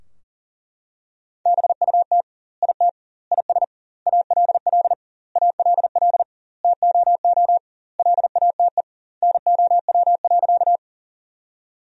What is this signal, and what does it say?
Morse code, heard as a keyed tone, about 37 wpm, 700 Hz. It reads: BUT IT IS ALL ALL TOO LATE NOW.
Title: 2x Word-Spacing - Sentences from Top 300 Words - 40wpm